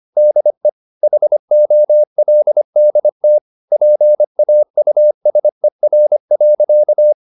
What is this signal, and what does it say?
Morse code, heard as a keyed tone, about 25 wpm, 600 Hz. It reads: DE HOLDT PAUSER.
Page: Ferie 25 wpm